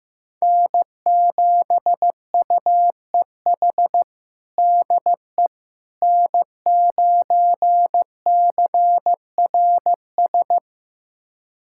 Morse code, heard as a keyed tone, about 15 wpm, 700 Hz. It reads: N7UEH DE N9CRS
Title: USA Callsign DE Callsign - 5 Characters - 15wpm